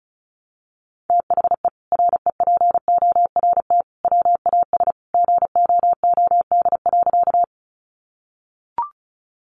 Morse code, heard as a keyed tone, about 34 wpm, 700 Hz. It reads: THE REPORT WAS GOOD.
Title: Sentences from Top 900 Words 35wpm